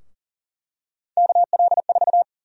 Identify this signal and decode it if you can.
Morse code, heard as a keyed tone, about 40 wpm, 700 Hz. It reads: KL4
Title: USA Call Sign Prefixes - 40wpm